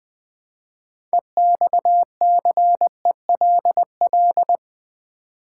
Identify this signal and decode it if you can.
Morse code, heard as a keyed tone, about 20 wpm, 700 Hz. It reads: EXCELL
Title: Build A Word - 3 Letters At A Time - 9-Letter Words - 20wpm